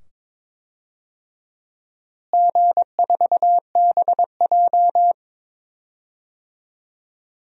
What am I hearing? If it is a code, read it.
Morse code, heard as a keyed tone, about 22 wpm, 700 Hz. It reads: G4BJ
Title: DXCC-100 Call Signs - 4 Characters - Repeated 2x - 22wpm